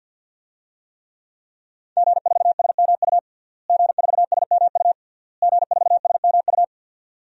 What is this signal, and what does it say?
Morse code, heard as a keyed tone, about 52 wpm, 700 Hz. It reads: G4SMU G4SMU G4SMU
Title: DXCC-100 Call Signs - 5 Characters - Repeated 3x - 50wpm